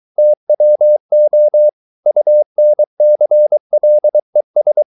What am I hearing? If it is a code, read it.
Morse code, heard as a keyed tone, about 23 wpm, 600 Hz. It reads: TWO UNCLES